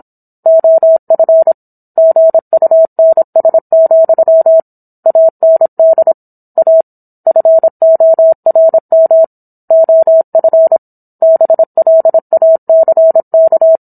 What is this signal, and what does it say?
Morse code, heard as a keyed tone, about 27 wpm, 650 Hz. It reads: OF GUNS, AND A FORM OF BLACK
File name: Book One / 17-Chapter 17 - 26 WPM